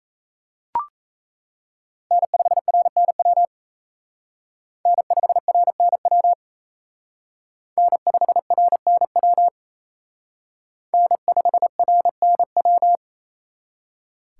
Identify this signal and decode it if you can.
Morse code, heard as a keyed tone, about 34 wpm, 700 Hz. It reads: N5RNW N5RNW N5RNW N5RNW
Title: Speed Racing - USA Call Signs 28wpm